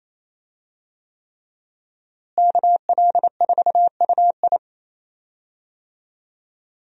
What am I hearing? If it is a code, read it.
Morse code, heard as a keyed tone, about 27 wpm, 700 Hz. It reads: KL4UI
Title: USA Call Signs - 5 Characters - Repeated 3x - 28wpm